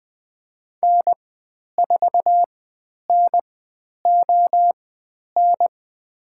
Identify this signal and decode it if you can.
Morse code, heard as a keyed tone, about 20 wpm, spaced as for 10 wpm, 700 Hz. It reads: N4NON
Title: 102 Lesson 17g - Letter u - Call Signs 20wpm